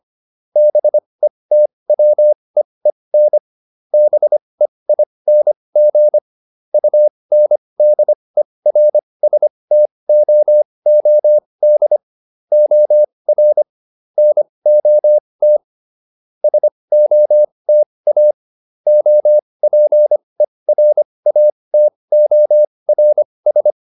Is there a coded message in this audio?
Yes: Morse code from 0.6 to 23.7 s, heard as a keyed tone, about 25 wpm, spaced as for 20 wpm, 600 Hz. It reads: BETWEEN BEING UNDERSTOOD OR NOT SOTA OPERATORS